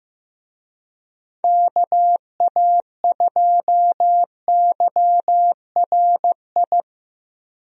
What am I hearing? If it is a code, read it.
Morse code, heard as a keyed tone, about 15 wpm, 700 Hz. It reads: KA2YRI
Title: USA Call Signs - 6 Characters - 15wpm